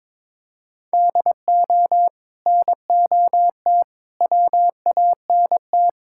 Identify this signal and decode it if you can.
Morse code, heard as a keyed tone, about 22 wpm, 700 Hz. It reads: DO NOT WANT
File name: practice / 22wpm / Build A Phrase - 5-Word Phrases - 22wpm